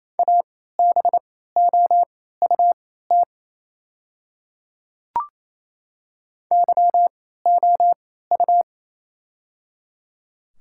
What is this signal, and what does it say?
Morse code, heard as a keyed tone, about 28 wpm, spaced as for 16 wpm, 690 Hz. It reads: ABOUT YOU
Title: 3x Character Spacing - Top 100 Words - 28wpm